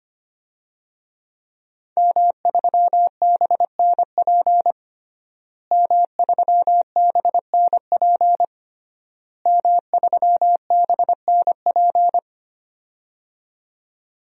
Morse code, heard as a keyed tone, about 25 wpm, 700 Hz. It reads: M3BNP M3BNP M3BNP
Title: DXCC-100 Call Signs - 5 Characters - Repeated 3x - 25wpm